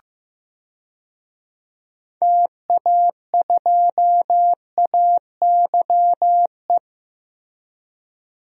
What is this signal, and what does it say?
Morse code, heard as a keyed tone, about 15 wpm, 700 Hz. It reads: TA2AYE
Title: DXCC-100 Call Signs - 6 Characters - Repeated 2x - 15wpm